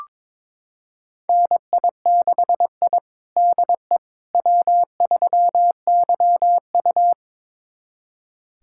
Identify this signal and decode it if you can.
Morse code, heard as a keyed tone, about 21 wpm, 700 Hz. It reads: NI6I DE W3YU
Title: USA Callsign DE Callsign - 4 Characters - 22wpm